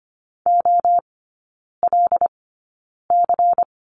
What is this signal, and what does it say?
Morse code, heard as a keyed tone, about 25 wpm, spaced as for 9 wpm, 700 Hz. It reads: OLC
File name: course / 25:10wpm / 093 Lesson 16e - Letter c - Three Characters 25wpm